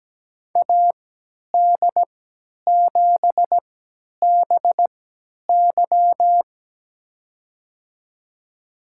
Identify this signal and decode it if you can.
Morse code, heard as a keyed tone, about 17 wpm, spaced as for 10 wpm, 700 Hz. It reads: AD7BY
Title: USA Call Signs - 3x Character Spacing - 17wpm